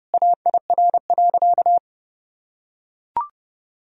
Morse code, heard as a keyed tone, about 32 wpm, 700 Hz. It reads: AIR.
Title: Sentences from Top 1000 Words - Review Part 1 - 30wpm